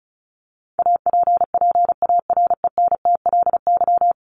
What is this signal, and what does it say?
Morse code, heard as a keyed tone, about 34 wpm, 700 Hz. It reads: APPARENTLY
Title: LY Suffix - Top 100 Words with Base Words 35wpm